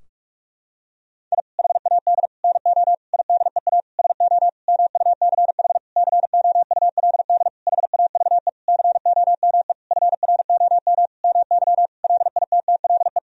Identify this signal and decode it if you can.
Morse code, heard as a keyed tone, about 46 wpm, 700 Hz. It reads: I HAD NO IDEA SO MUCH COULD HAVE COME FROM MY LITTLE
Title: Sentences from Top 1000 Words - Review Part 4 - 45wpm